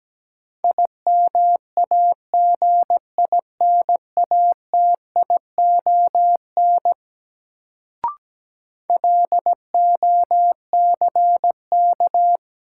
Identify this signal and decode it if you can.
Morse code, heard as a keyed tone, about 17 wpm, 700 Hz. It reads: IMAGINATION LOCK
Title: Sentences from Top 1300 Words - New Word Drill - 17wpm